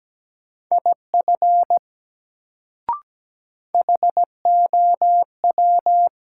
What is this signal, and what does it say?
Morse code, heard as a keyed tone, about 17 wpm, 700 Hz. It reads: IF HOW
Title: Sentences from Top 100 Words - New Word Drill - 17wpm